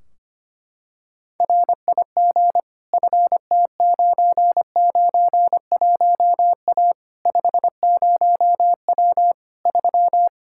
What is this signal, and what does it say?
Morse code, heard as a keyed tone, about 25 wpm, 700 Hz. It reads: RIG FT991A 50W 3
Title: Practice QSO - Exchange 2 - Info - 25wpm